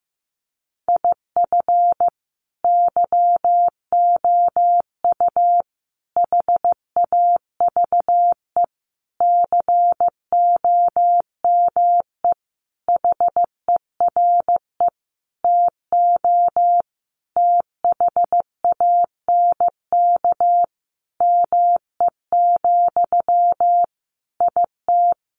Morse code, heard as a keyed tone, about 15 wpm, 700 Hz. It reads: IF YOU HAVE COME HERE TO THANK ME, IT
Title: Sentences from Top 800 Words 15wpm